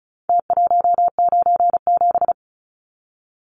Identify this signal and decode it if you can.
Morse code, heard as a keyed tone, about 34 wpm, 700 Hz. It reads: T197
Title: Numbers - Contest Style - 4-Digit 35wpm